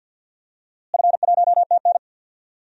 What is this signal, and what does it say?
Morse code, heard as a keyed tone, about 45 wpm, 700 Hz. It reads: F1TN
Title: DXCC-100 Call Signs - 4 Characters - Repeated 2x - 50wpm